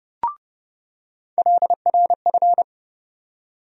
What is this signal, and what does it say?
Morse code, heard as a keyed tone, about 30 wpm, 700 Hz. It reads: LRF
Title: USA Call Sign Postfixes - 3 Letters 30wpm